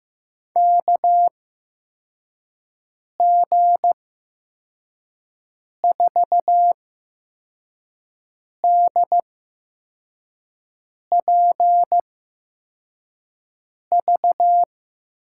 Morse code, heard as a keyed tone, about 15 wpm, spaced as for 4 wpm, 700 Hz. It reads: KG4DPV
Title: USA Call Signs - 8x Character Spacing - 15wpm